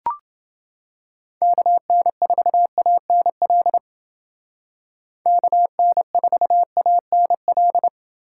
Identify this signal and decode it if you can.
Morse code, heard as a keyed tone, about 29 wpm, 700 Hz. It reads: KN4ANL KN4ANL